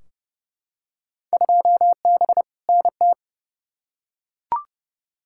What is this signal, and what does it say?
Morse code, heard as a keyed tone, about 29 wpm, 700 Hz. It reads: 2B NT
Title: Field Day Exchanges Encore - 30wpm